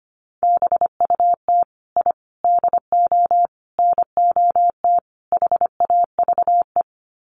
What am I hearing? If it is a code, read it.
Morse code, heard as a keyed tone, about 25 wpm, 700 Hz. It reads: BUT I DO NOT HAVE
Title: Sets of 5 Words Encore 25wpm